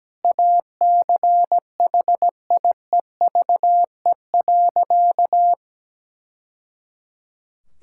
Morse code, heard as a keyed tone, about 17 wpm, 700 Hz. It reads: ACHIEVE.
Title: Sentences from Top 1000 Words - Review Part 6 - 17wpm